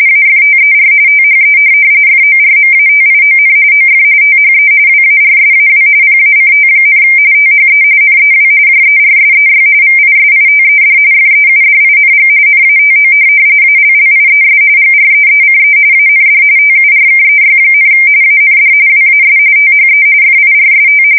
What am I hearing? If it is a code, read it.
RTTY, about 45.45 baud. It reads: AFTER THE FORMATION OF THE WARSAW PACT, THE NEW JERSEY VISITED FEUDING NATO ALLIES GREECE AND TURKEY, AND WAS THE FIRST
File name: fsk170 / USS-New-Jersey